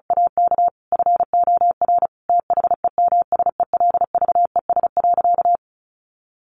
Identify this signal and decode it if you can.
Morse code, heard as a keyed tone, about 34 wpm, 700 Hz. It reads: AK FOR THEMSELVES.
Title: Sentences from Top 1400 Words 35wpm